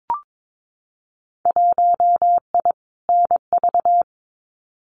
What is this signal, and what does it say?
Morse code, heard as a keyed tone, about 22 wpm, 700 Hz. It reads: 1I NV
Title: Winter Field Day Exchanges Encore 22wpm